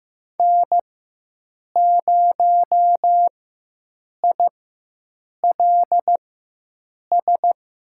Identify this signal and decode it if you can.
Morse code, heard as a keyed tone, about 15 wpm, spaced as for 7 wpm, 700 Hz. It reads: N0ILS
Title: USA Call Signs - 5 Characters - 4x Character Spacing - 15wpm